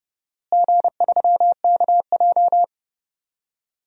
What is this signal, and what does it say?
Morse code, heard as a keyed tone, about 30 wpm, 700 Hz. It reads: G3KJ
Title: DXCC-100 Call Signs - 4 Characters - Repeated 2x - 30wpm